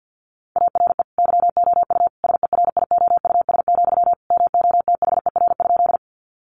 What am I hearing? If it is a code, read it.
Morse code, heard as a keyed tone, about 49 wpm, 700 Hz. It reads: ARE YOU SERIOUS, MOTHER?
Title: Sentences from Top 1100 Words 50wpm